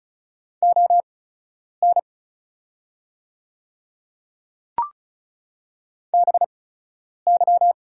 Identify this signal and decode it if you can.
Morse code, heard as a keyed tone, about 36 wpm, spaced as for 10 wpm, 700 Hz. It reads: ON BY
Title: 8x Character Spacing - Top 100 Words - 35wpm